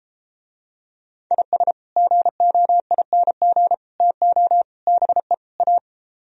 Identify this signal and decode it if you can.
Morse code, heard as a keyed tone, about 33 wpm, 700 Hz. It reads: IS GOING TO BE A